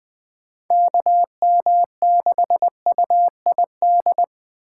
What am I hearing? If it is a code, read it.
Morse code, heard as a keyed tone, about 20 wpm, 700 Hz. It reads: KM6UID
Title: Speed Racing - USA 2x3 Call Signs 17wpm